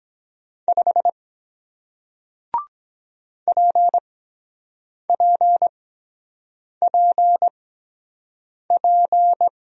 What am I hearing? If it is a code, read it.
Morse code, heard as a keyed tone, about 23 wpm, spaced as for 7 wpm, 700 Hz. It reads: 5 PPPP